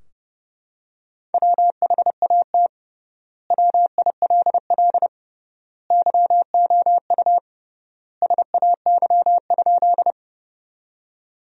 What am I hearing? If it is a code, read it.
Morse code, heard as a keyed tone, about 30 wpm, 700 Hz. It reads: WHAT WILL YOU SAY?
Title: 3x Word-Spacing - Sentences from Top 200 Words - 30wpm